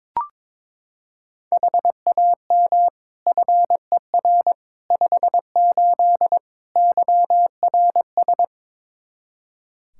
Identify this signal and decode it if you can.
Morse code, heard as a keyed tone, about 21 wpm, 700 Hz. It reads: HAM FER 58 YRS
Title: Practice QSO - Exchange 3 - More Info - 22wpm